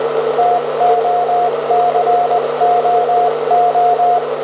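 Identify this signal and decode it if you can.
Morse code, heard as a keyed tone, about 20 wpm, 710 Hz, with radio noise. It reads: TOCOO
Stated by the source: 160m signal on 1854 kHz, 100W into a 60m antenna, received on an R-311 with an indoor antenna and microphone, 400Hz filter, noise S5.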